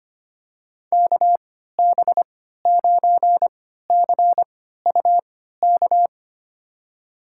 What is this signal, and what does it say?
Morse code, heard as a keyed tone, about 25 wpm, spaced as for 14 wpm, 700 Hz. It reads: KB9CUK
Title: USA Call Signs - 6 Characters - 3x Character Spacing - 25wpm